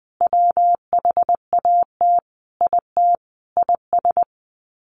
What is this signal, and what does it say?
Morse code, heard as a keyed tone, about 20 wpm, 700 Hz. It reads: WHAT IT IS
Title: Sets of 5 Words Encore 20wpm